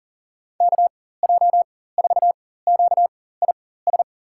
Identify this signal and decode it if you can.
Morse code, heard as a keyed tone, about 40 wpm, spaced as for 19 wpm, 700 Hz. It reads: KJ4QIS
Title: USA Call Signs - 6 Characters - 4x Character Spacing - 40wpm